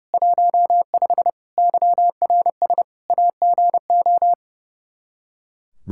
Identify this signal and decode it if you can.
Morse code, heard as a keyed tone, about 32 wpm, 700 Hz. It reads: 15 YRS AGO
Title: K1USN SST QSO Practice 30wpm